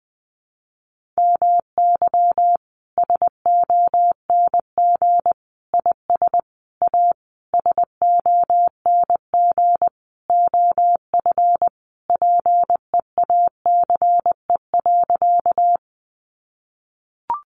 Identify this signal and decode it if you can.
Morse code, heard as a keyed tone, about 20 wpm, 700 Hz. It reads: MY SONG IS A SONG OF PEACE.
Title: Sentences from Top 1500 Words 20wpm